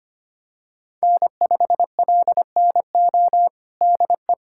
Morse code, heard as a keyed tone, about 25 wpm, 700 Hz. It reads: N5LNO DE
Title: USA Callsign DE Callsign - 5 Characters - 25wpm